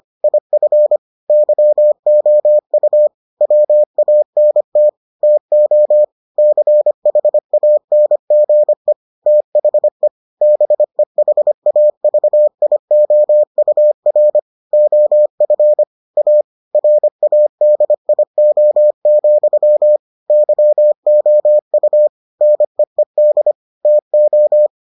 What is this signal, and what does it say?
Morse code, heard as a keyed tone, about 25 wpm, 600 Hz. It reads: IF YOU WANT TO CHANGE THE BEHAVIOUR OF A RADIO, YOU NEED TO